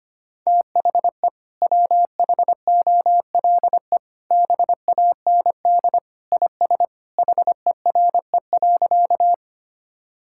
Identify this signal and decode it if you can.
Morse code, heard as a keyed tone, about 25 wpm, 700 Hz. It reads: THE WHOLE BAND IS HERE.
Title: Sentences from Top 1600 Words 25wpm